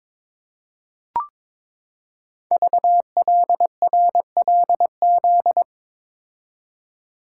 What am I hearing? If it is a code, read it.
Morse code, heard as a keyed tone, about 22 wpm, 700 Hz. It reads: VLRLZ
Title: Code Groups - 5 Letters - 22wpm